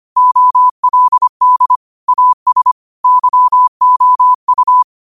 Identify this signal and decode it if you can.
Morse code, heard as a keyed tone, about 25 wpm, 1000 Hz. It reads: OLD AS YOU
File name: qotdfiles / 25wpm